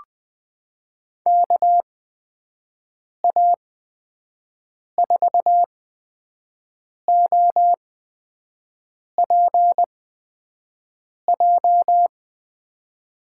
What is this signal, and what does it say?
Morse code, heard as a keyed tone, about 20 wpm, spaced as for 5 wpm, 700 Hz. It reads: KA4OPJ
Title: USA Call Signs - 8x Character Spacing - 20wpm